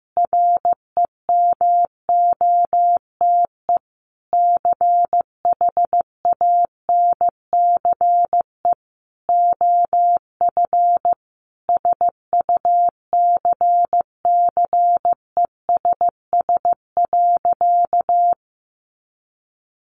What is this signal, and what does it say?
Morse code, heard as a keyed tone, about 15 wpm, 700 Hz. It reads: REMOTE CHANCE OF SUCCESS.
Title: Sentences from Top 1300 Words 15wpm